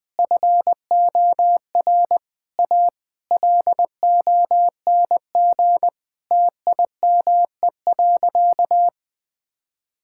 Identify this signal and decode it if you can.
Morse code, heard as a keyed tone, about 20 wpm, 700 Hz. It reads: FOR A LONG TIME.
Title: Sentences with 8-Word Phrases - 20wpm